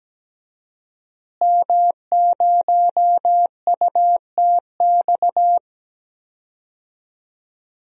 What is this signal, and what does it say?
Morse code, heard as a keyed tone, about 17 wpm, 700 Hz. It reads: M0UTX
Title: DXCC-100 Call Signs - 5 Characters - Repeated 2x - 17wpm